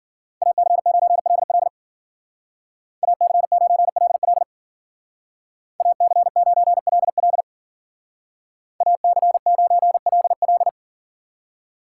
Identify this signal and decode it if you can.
Morse code, heard as a keyed tone, about 52 wpm, 700 Hz. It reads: AC9LL AC9LL AC9LL AC9LL